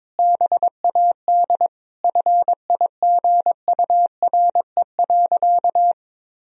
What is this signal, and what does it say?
Morse code, heard as a keyed tone, about 22 wpm, 700 Hz. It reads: BAD FIGURE.
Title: Sentences from Top 1000 Words - Review Part 2 - 22wpm